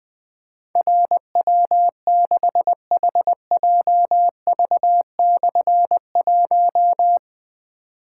Morse code, heard as a keyed tone, about 20 wpm, 700 Hz. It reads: RW6HJV/1
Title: Difficult Call Signs 20wpm